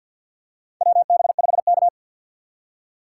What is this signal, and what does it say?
Morse code, heard as a keyed tone, about 47 wpm, 700 Hz. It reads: WB5K
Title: K1USN SST QSO Practice - 50wpm